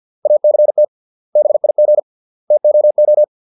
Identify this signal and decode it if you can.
Morse code, heard as a keyed tone, about 50 wpm, 600 Hz. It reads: ACT BIG TOO